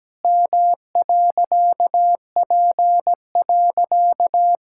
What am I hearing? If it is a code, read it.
Morse code, heard as a keyed tone, about 17 wpm, 700 Hz. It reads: M.P.
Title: Pangram Challenge v2 - 17wpm